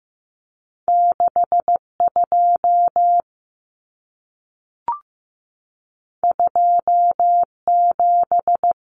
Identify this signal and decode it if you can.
Morse code, heard as a keyed tone, about 15 wpm, 700 Hz. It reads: 62 27
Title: Common Cut Numbers - 2-Digit 15wpm